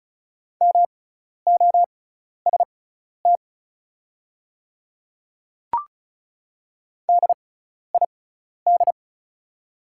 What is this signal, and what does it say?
Morse code, heard as a keyed tone, about 36 wpm, spaced as for 12 wpm, 690 Hz. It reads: MOST DID
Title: 6x Character Spacing - Top 100 Words - 35wpm